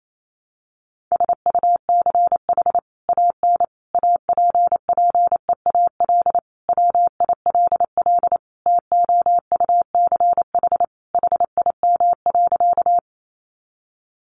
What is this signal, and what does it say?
Morse code, heard as a keyed tone, about 28 wpm, 700 Hz. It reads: SUCH AN APPEAL WILL TOUCH HIM.
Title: Sentences from Top 1300 Words 28wpm